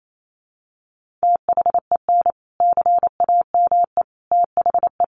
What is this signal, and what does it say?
Morse code, heard as a keyed tone, about 28 wpm, 700 Hz. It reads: THEN CAME THE
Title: Sentences from Top 1500 Words 28wpm